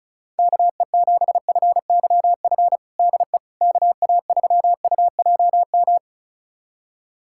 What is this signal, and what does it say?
Morse code, heard as a keyed tone, about 36 wpm, 700 Hz. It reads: KE7FYF DE KA3UJM
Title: USA Callsign DE Callsign - 6 Characters - 35wpm